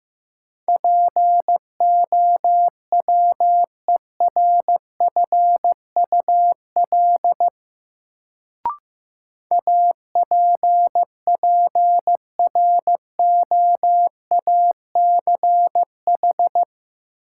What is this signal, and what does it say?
Morse code, heard as a keyed tone, about 15 wpm, 700 Hz. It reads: POWERFUL APPROACH
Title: Sentences from Top 1000 Words - Review Part 6 - New Word Drill - 15wpm